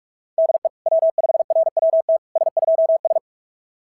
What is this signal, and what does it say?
Morse code, heard as a keyed tone, about 44 wpm, 690 Hz. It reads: DE W5RWT S2S
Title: SOTA QSO Practice 45wpm